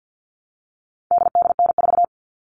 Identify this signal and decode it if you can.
Morse code, heard as a keyed tone, about 50 wpm, 700 Hz. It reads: DDN4
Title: All Cut Numbers - 4-Digit 50wpm